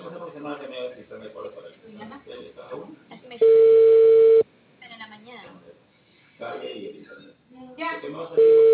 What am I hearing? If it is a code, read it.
Ringback tone (China, 450 Hz), with speech.